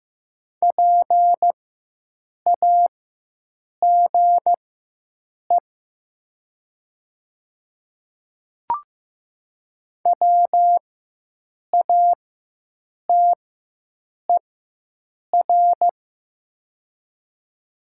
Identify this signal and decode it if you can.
Morse code, heard as a keyed tone, about 15 wpm, spaced as for 7 wpm, 700 Hz. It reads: PAGE WATER
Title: 4x Character Spacing - Top 500 Words - 15wpm